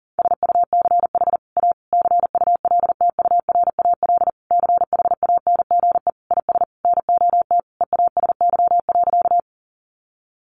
Morse code, heard as a keyed tone, about 40 wpm, 700 Hz. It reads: SUCH A CULTURAL CHANGE IS NOT EASY.
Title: Sentences from Top 1500 Words 40wpm